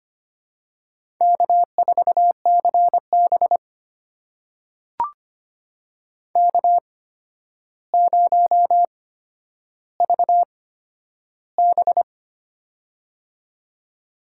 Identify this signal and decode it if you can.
Morse code, heard as a keyed tone, about 25 wpm, spaced as for 11 wpm, 700 Hz. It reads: K4CB K0VB
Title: USA Call Signs - 4 Characters - 8x Character Spacing - 25wpm